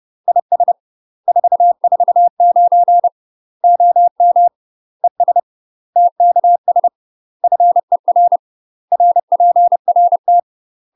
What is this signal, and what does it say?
Morse code, heard as a keyed tone, about 30 wpm, 700 Hz. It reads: IS 449 OM ES TKS FER RPRT
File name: UK MASTER QSOs TRAINING / 30WPM / UK MASTER QSOs 0047